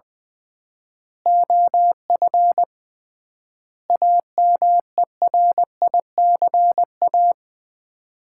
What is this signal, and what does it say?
Morse code, heard as a keyed tone, about 20 wpm, 700 Hz. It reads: OF AMERICA
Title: 3x Word-Spacing - Sets of 5 Words - 20wpm